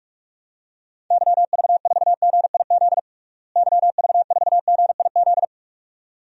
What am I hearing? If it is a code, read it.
Morse code, heard as a keyed tone, about 46 wpm, 700 Hz. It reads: YV4GIZ YV4GIZ
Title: DXCC-100 Call Signs - 6 Characters - Repeated 2x - 45wpm